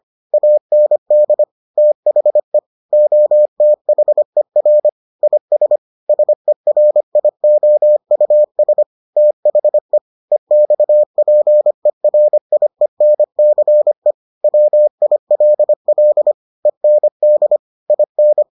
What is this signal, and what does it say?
Morse code, heard as a keyed tone, about 25 wpm, 600 Hz. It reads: AND THE OTHER IS SERIOUS THE EXPERIENCE WILL END IN